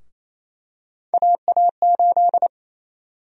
Morse code, heard as a keyed tone, about 28 wpm, 700 Hz. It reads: AA8
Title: USA Call Sign Prefixes - 28wpm